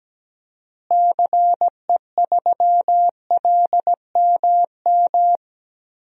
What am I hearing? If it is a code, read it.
Morse code, heard as a keyed tone, about 17 wpm, 700 Hz. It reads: CE3LMM
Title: DXCC-100 Call Signs - 6 Characters - Repeated 3x - 17wpm